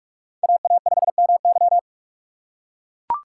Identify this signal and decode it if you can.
Morse code, heard as a keyed tone, about 44 wpm, 700 Hz. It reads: AA5MY